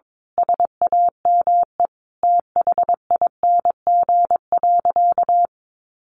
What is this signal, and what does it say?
Morse code, heard as a keyed tone, about 22 wpm, 700 Hz. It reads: SAME THING.
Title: Sentences from Top 1100 Words 22wpm